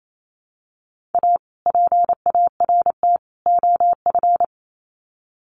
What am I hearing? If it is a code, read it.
Morse code, heard as a keyed tone, about 28 wpm, 700 Hz. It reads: A PART OF
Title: Sets of 3 Words 28wpm